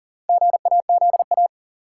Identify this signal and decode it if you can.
Morse code, heard as a keyed tone, about 40 wpm, 700 Hz. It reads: GAZA
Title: Sets of 5 Words - New Word Drill - 40wpm